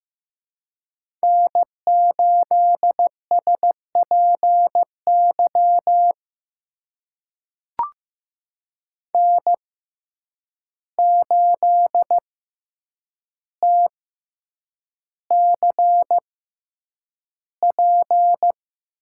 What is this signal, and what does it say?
Morse code, heard as a keyed tone, about 15 wpm, spaced as for 8 wpm, 700 Hz. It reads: N8SPY N8TCP
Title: USA Call Signs - 5 Characters - 6x Character Spacing - 15wpm